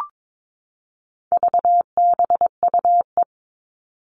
Morse code, heard as a keyed tone, about 22 wpm, 700 Hz. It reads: VBUE
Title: All Cut Numbers - 4-Digit 22wpm